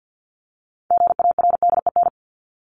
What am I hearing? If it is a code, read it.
Morse code, heard as a keyed tone, about 50 wpm, 700 Hz. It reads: GARDEN
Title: Top 501-750 Words 50wpm